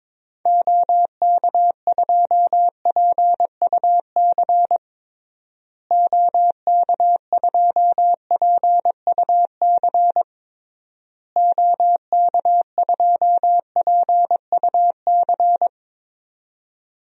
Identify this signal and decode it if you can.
Morse code, heard as a keyed tone, about 22 wpm, 700 Hz. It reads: OK2PUC OK2PUC OK2PUC
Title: DXCC-100 Call Signs - 6 Characters - Repeated 3x - 22wpm